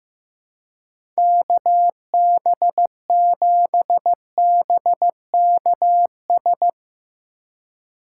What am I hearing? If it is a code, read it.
Morse code, heard as a keyed tone, about 15 wpm, 700 Hz. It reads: KB7BKS
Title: USA Call Signs - 6 Characters - 2x Character Spacing - 15wpm